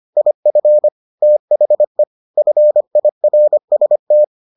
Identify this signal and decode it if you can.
Morse code, heard as a keyed tone, about 25 wpm, 600 Hz. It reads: IF THE FIRST